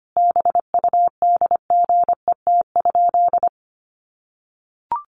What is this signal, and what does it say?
Morse code, heard as a keyed tone, about 25 wpm, 700 Hz. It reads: BUDGET?
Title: Sentences from Top 1200 Words 25wpm